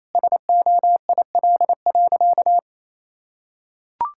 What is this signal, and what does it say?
Morse code, heard as a keyed tone, about 28 wpm, 700 Hz. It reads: SOIL.
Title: Sentences from Top 1000 Words - Review Part 5 - 28wpm